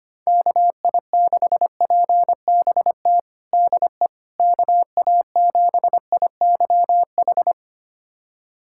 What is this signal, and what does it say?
Morse code, heard as a keyed tone, about 25 wpm, 700 Hz. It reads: KI6PBT DE KA7IYH
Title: USA Callsign DE Callsign - 6 Characters - 25wpm